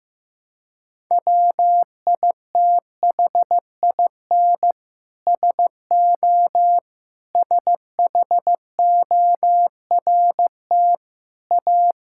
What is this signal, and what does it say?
Morse code, heard as a keyed tone, about 15 wpm, 700 Hz. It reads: WITHIN SO SHORT A
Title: Sentences from Top 1000 Words - Review Part 5 - 15wpm